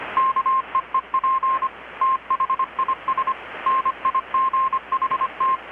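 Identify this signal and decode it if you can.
Morse code, heard as a keyed tone, about 25 wpm, 1060 Hz, with radio noise. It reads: KEEP THIS NIGHT
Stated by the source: Morse from KPH's Night of Nights XX maritime radiotelegraphy event, received in Chiapas, Mexico about 3,800 km away on a Kenwood R-600 with a grounded 10 m long-wire antenna.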